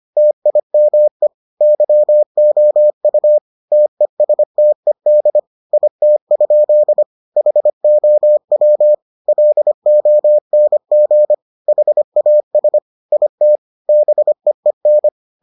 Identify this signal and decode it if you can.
Morse code, heard as a keyed tone, about 25 wpm, 600 Hz. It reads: TIME YOU TESTED IT? HOW LONG HAS IT BEEN